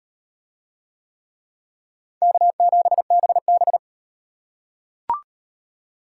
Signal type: Morse code

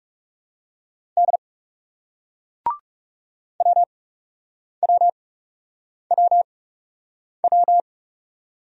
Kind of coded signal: Morse code

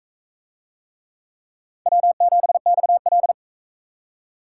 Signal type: Morse code

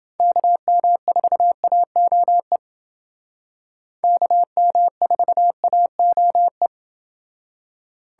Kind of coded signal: Morse code

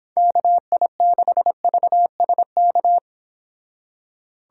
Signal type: Morse code

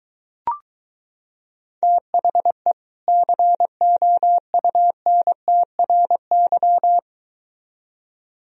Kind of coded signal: Morse code

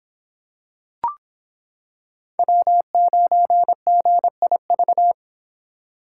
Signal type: Morse code

